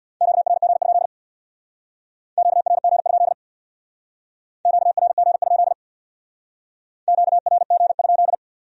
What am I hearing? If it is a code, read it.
Morse code, heard as a keyed tone, about 61 wpm, 700 Hz. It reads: QRG? QRG? QRG? QRG?
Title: Speed-Racing Q-Codes - 50wpm